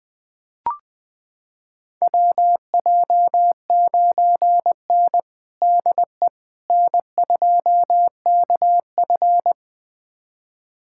Morse code, heard as a keyed tone, about 21 wpm, 700 Hz. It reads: WJ9N DE N2KF
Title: USA Callsign DE Callsign - 4 Characters - 20wpm